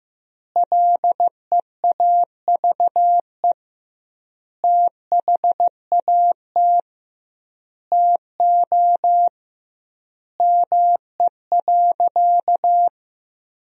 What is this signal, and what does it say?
Morse code, heard as a keyed tone, about 15 wpm, 700 Hz. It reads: LEAVE THAT TO ME.
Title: 2x Word-Spacing - Sentences from Top 400 Words - 15wpm